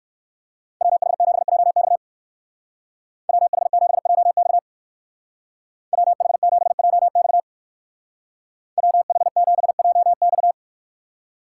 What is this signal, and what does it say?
Morse code, heard as a keyed tone, about 59 wpm, 700 Hz. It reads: WH7JX WH7JX WH7JX WH7JX